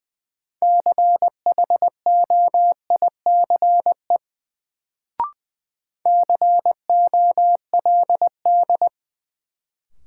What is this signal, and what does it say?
Morse code, heard as a keyed tone, about 20 wpm, 700 Hz. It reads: CHOICE COLD
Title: Sets of 5 Words - New Word Drill - 20wpm